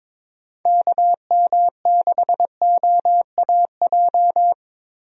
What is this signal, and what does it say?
Morse code, heard as a keyed tone, about 22 wpm, 700 Hz. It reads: KM6OAJ
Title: USA Call Signs - 6 Characters - Repeated 3x - 22wpm